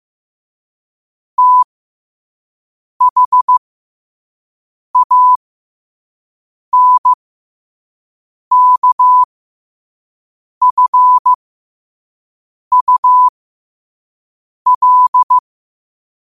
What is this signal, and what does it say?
Morse code, heard as a keyed tone, about 15 wpm, spaced as for 5 wpm, 1000 Hz. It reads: THANKFUL
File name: qotdfiles / 5wpm